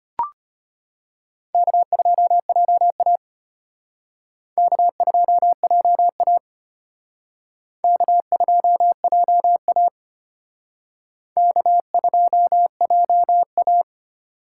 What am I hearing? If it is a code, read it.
Morse code, heard as a keyed tone, about 32 wpm, 700 Hz. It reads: K2JA K2JA K2JA K2JA